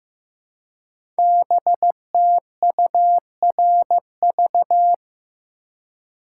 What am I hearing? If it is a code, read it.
Morse code, heard as a keyed tone, about 15 wpm, 700 Hz. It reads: BTURV
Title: Code Groups - 5 Letters - 15wpm